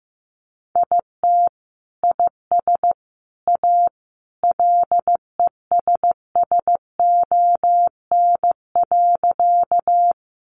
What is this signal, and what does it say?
Morse code, heard as a keyed tone, about 15 wpm, 700 Hz. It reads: IT IS A LESSON.
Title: Sentences from Top 1200 Words 15wpm